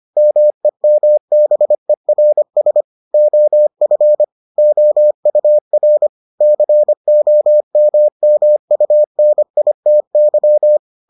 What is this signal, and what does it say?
Morse code, heard as a keyed tone, about 25 wpm, 600 Hz. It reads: MEMBERS OF OUR COMMUNITY